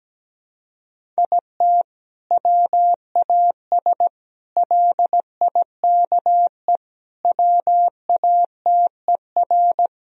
Morse code, heard as a keyed tone, about 17 wpm, 700 Hz. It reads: IT WAS LIKE WATER
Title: Sentences from Top 1000 Words - Review Part 4 - 17wpm